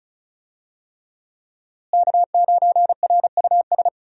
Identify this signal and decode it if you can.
Morse code, heard as a keyed tone, about 34 wpm, 700 Hz. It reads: K9RUS
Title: USA Call Signs - 5 Characters - Repeated 3x - 35wpm